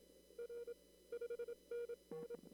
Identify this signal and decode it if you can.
Morse code, heard as a keyed tone, about 26 wpm, 470 Hz. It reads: R 5NN
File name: CW / kt5j wpx cw-op-k5pi